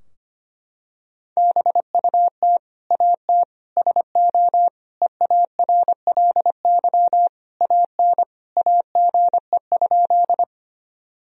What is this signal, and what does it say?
Morse code, heard as a keyed tone, about 25 wpm, 700 Hz. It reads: BUT AT SO EARLY AN AGE?
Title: Sentences from Top 1000 Words - Review Part 3 - 25wpm